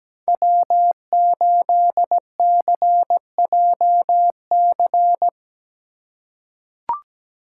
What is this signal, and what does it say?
Morse code, heard as a keyed tone, about 17 wpm, 700 Hz. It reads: W8CJC
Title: USA Call Signs - 5 Characters - 2x Character Spacing - 17wpm